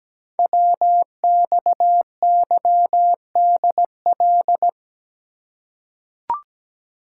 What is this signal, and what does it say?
Morse code, heard as a keyed tone, about 17 wpm, 700 Hz. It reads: WXYDL
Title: Code Groups - 5 Letters - 17wpm